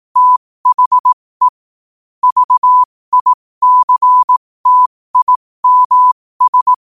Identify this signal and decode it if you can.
Morse code, heard as a keyed tone, about 18 wpm, spaced as for 15 wpm, 1000 Hz. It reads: THE VICTIMS
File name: qotdfiles / 15wpm